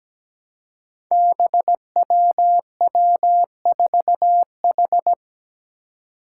Morse code, heard as a keyed tone, about 17 wpm, 700 Hz. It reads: BWW4H
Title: Five Challenging Letter-Number Combinations 17wpm